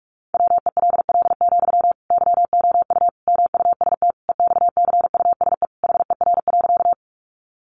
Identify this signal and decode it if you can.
Morse code, heard as a keyed tone, about 44 wpm, 700 Hz. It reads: WELL, YOU MUST EXCUSE HER.
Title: Sentences from Top 700 Words 45wpm